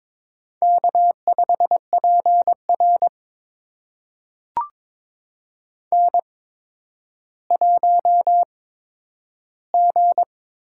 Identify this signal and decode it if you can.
Morse code, heard as a keyed tone, about 23 wpm, spaced as for 11 wpm, 700 Hz. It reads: K5PR N1G
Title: USA Call Signs - 8x Character Spacing - 22wpm